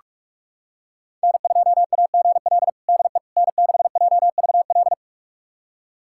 Morse code, heard as a keyed tone, about 44 wpm, 700 Hz. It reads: N2AGL DE N6JVL
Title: USA Callsign DE Callsign - 5 Characters - 45wpm